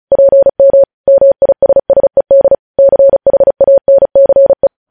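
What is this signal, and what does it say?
Morse code, heard as a keyed tone, about 36 wpm, 550 Hz. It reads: PM MISSED CHANCE
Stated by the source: Morse code practice of news headlines at 35 wpm, opening with vvv.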